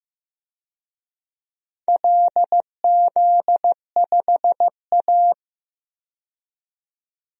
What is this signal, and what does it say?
Morse code, heard as a keyed tone, about 15 wpm, 700 Hz. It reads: LZ5A
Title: DXCC-100 Call Signs - 4 Characters - 15wpm